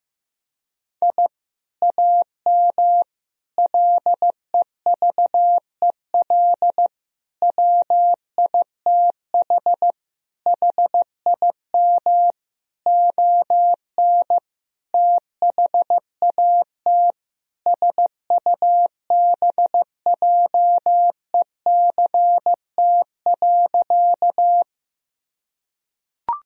Morse code, heard as a keyed tone, about 15 wpm, 700 Hz. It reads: I AM LEVEL WITH HIM ON THAT SUBJECT.
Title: Sentences from Top 1000 Words - Review Part 4 - 15wpm